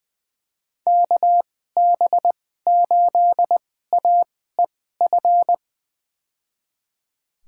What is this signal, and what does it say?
Morse code, heard as a keyed tone, about 20 wpm, spaced as for 14 wpm, 700 Hz. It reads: KB8AEF
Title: USA Call Signs - 6 Characters - 2x Character Spacing - 20wpm